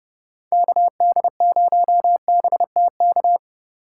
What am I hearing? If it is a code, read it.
Morse code, heard as a keyed tone, about 30 wpm, 700 Hz. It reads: KD0BTK